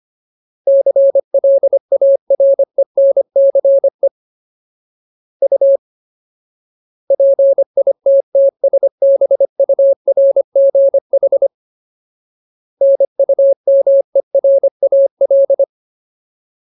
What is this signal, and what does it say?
Morse code, heard as a keyed tone, about 25 wpm, 550 Hz. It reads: CLARENCE U PITTSBURGH NUMERAL
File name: Ultimate Word List / 25WPM Ultimate